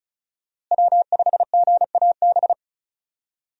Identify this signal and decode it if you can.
Morse code, heard as a keyed tone, about 36 wpm, 700 Hz. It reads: W5GAB
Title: USA Call Signs - 5 Characters - Repeated 2x - 35wpm